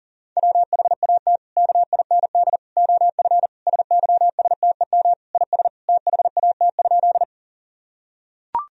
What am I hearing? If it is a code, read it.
Morse code, heard as a keyed tone, about 40 wpm, 700 Hz. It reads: WHAT KIND OF SYSTEM IS THAT?
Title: Sentences from Top 1000 Words - Review Part 3 - 40wpm